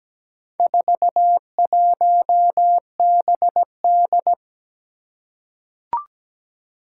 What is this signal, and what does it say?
Morse code, heard as a keyed tone, about 17 wpm, 700 Hz. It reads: 41BD
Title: Four Challenging Letter-Number Combinations 17wpm